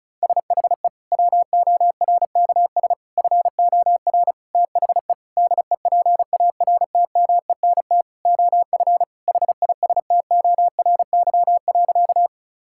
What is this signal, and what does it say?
Morse code, heard as a keyed tone, about 34 wpm, 700 Hz. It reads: SHE WORKS FOR THE DEPARTMENT OF HISTORY.
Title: Sentences from Top 1000 Words - Review Part 5 - 35wpm